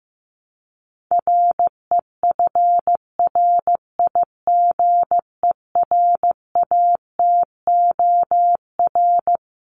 Morse code, heard as a keyed tone, about 15 wpm, 700 Hz. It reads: REFRIGERATOR
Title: Top 1501-1750 Words 15wpm